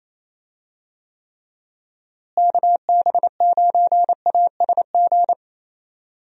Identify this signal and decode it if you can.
Morse code, heard as a keyed tone, about 28 wpm, 700 Hz. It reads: KB9ASG
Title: USA Call Signs - 6 Characters - Repeated 3x - 28wpm